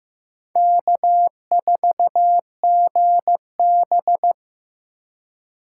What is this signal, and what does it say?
Morse code, heard as a keyed tone, about 15 wpm, 700 Hz. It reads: K4GB
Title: USA Call Signs - 4 Characters - 3x Character Spacing - 15wpm